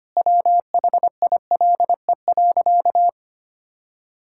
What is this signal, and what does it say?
Morse code, heard as a keyed tone, about 25 wpm, 700 Hz. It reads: WHILE.
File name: practice / 25wpm / Sentences with 4-Word Phrases - 25wpm